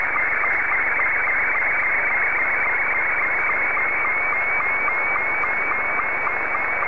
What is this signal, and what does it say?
SSTV (slow-scan television), Scottie 2.